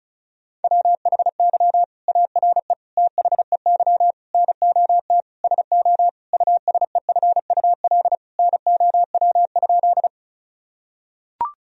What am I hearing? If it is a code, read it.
Morse code, heard as a keyed tone, about 34 wpm, 700 Hz. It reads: WHY ARE THEY NOT SO USEFUL NOW?
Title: Sentences from Top 1000 Words - Review Part 4 - 35wpm